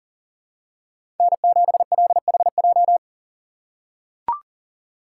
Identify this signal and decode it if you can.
Morse code, heard as a keyed tone, about 40 wpm, 700 Hz. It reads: N7LHJ
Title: USA Call Signs - 4x Character Spacing - 40wpm